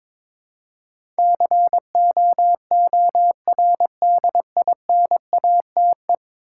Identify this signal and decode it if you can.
Morse code, heard as a keyed tone, about 22 wpm, 700 Hz. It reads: COORDINATE
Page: morse-code-ninja-practice-22wpm